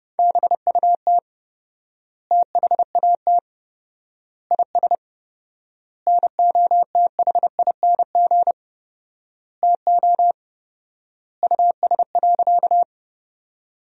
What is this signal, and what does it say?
Morse code, heard as a keyed tone, about 30 wpm, 700 Hz. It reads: BUT THAT IS NOTHING TO US.
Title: 4x Word-Spacing - Sentences from Top 500 Words - 30wpm